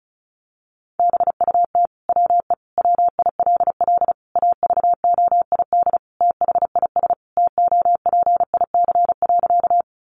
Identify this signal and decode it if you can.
Morse code, heard as a keyed tone, about 34 wpm, 700 Hz. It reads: BUT WE WILL AVOID THIS TOPIC.
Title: Sentences from Top 1200 Words 35wpm